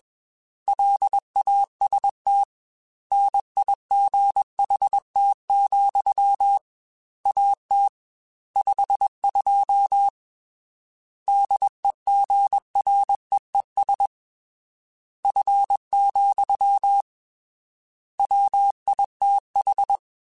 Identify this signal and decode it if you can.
Morse code, heard as a keyed tone, about 21 wpm, 800 Hz. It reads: LAST NIGHT, AT 52 DEGREES F, WITH